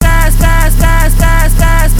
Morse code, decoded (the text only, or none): none